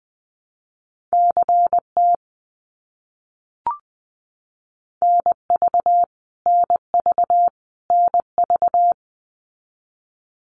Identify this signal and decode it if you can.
Morse code, heard as a keyed tone, about 20 wpm, 700 Hz. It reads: CT NV NV NV
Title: US States Repeated 3x 20wpm